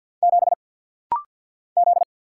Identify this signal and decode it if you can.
Morse code, heard as a keyed tone, about 50 wpm, 700 Hz. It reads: 7 Z
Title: Single Letter-Number - Rapid-Fire 50wpm